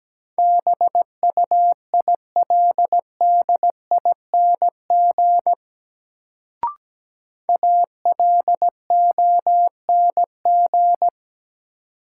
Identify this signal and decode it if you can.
Morse code, heard as a keyed tone, about 17 wpm, 700 Hz. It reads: BUILDING ALONG
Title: Sentences from Top 700 Words - New Word Drill - 17wpm